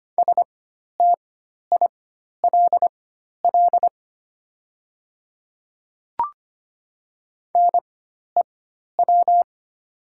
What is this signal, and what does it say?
Morse code, heard as a keyed tone, about 25 wpm, spaced as for 12 wpm, 700 Hz. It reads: STILL NEW
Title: 4x Character Spacing - Top 500 Words - 25wpm